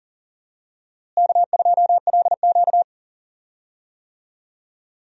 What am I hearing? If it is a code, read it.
Morse code, heard as a keyed tone, about 40 wpm, 700 Hz. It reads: K2LQ
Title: USA Call Signs - 4 Characters - Repeated 3x - 40wpm